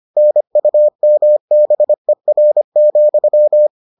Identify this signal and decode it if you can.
Morse code, heard as a keyed tone, about 25 wpm, 600 Hz. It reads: NUMBER,